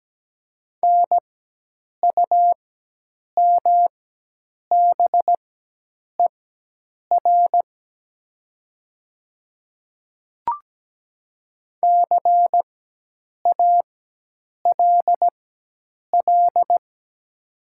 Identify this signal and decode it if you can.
Morse code, heard as a keyed tone, about 17 wpm, spaced as for 8 wpm, 700 Hz. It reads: NUMBER CALL
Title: 4x Character Spacing - Top 100 Words - 17wpm